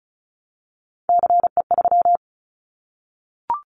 Morse code, heard as a keyed tone, about 34 wpm, 700 Hz. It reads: CE3
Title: Callsign Prefixes for Top 100 DXCC - 35wpm